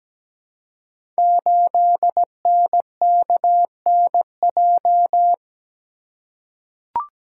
Morse code, heard as a keyed tone, about 17 wpm, 700 Hz. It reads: 8NKNJ
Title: Code Groups - 5 Letter:Numbers - 17wpm